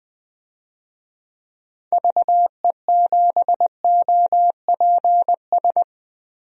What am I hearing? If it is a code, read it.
Morse code, heard as a keyed tone, about 20 wpm, 700 Hz. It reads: VE7OPS
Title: DXCC-100 Call Signs - 6 Characters - 20wpm